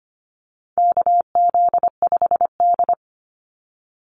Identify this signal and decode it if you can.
Morse code, heard as a keyed tone, about 25 wpm, 700 Hz. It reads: KZ5D
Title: Call Signs Repeated 2x - 25wpm